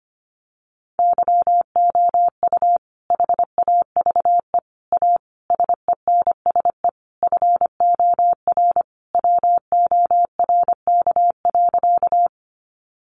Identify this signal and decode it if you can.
Morse code, heard as a keyed tone, about 25 wpm, 700 Hz. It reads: YOU HAVE A SENSE FOR WORK.
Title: Sentences from Top 700 Words 25wpm